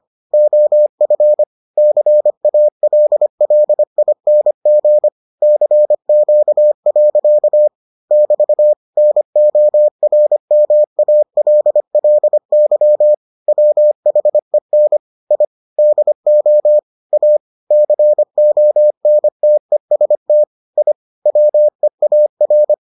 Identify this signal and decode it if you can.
Morse code, heard as a keyed tone, about 25 wpm, 600 Hz. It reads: OF CALLING CQ. <BT> NORMALLY WHEN I DO A CONTEST I WEAR